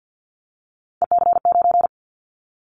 Morse code, heard as a keyed tone, about 50 wpm, 700 Hz. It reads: EC9
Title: Callsign Prefixes for Top 100 DXCC - 50wpm